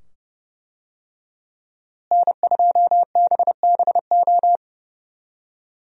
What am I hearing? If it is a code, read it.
Morse code, heard as a keyed tone, about 31 wpm, 700 Hz. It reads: N2BBO
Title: USA Call Signs - 5 Characters - Repeated 2x - 30wpm